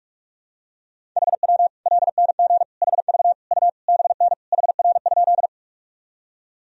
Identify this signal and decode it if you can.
Morse code, heard as a keyed tone, about 46 wpm, 700 Hz. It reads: HW LNG HV U BN HR?
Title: Sentences from Top 200 Words and Abbreviations 45wpm